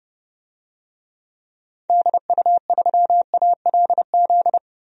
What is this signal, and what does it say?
Morse code, heard as a keyed tone, about 30 wpm, 700 Hz. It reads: DU3ALZ
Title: DXCC-100 Call Signs - 6 Characters - 30wpm